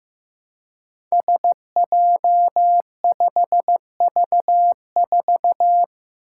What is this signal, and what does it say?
Morse code, heard as a keyed tone, about 15 wpm, 700 Hz. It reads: SJ5V4
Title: Five Challenging Letter-Number Combinations 15wpm